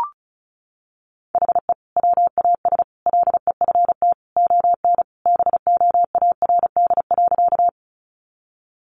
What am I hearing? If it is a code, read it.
Morse code, heard as a keyed tone, about 34 wpm, 700 Hz. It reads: HE WAS LEFT ON BOARD.
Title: Sentences from Top 700 Words 35wpm